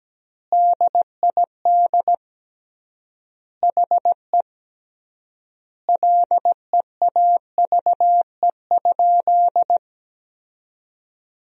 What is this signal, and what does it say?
Morse code, heard as a keyed tone, about 17 wpm, 700 Hz. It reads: DID HE LEAVE?
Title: 3x Word-Spacing - Sentences from Top 400 Words - 17wpm